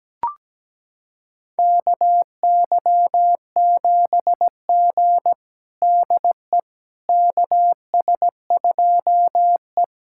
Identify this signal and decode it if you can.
Morse code, heard as a keyed tone, about 17 wpm, 700 Hz. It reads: KY7G DE KS2E
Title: USA Callsign DE Callsign - 4 Characters - 17wpm